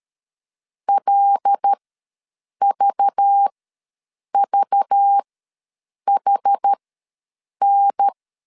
Morse code, heard as a keyed tone, about 13 wpm, spaced as for 7 wpm, 800 Hz. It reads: LVVHN
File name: LEZIONI / 5 LBVC / 5 LBV ws 12wpm 3015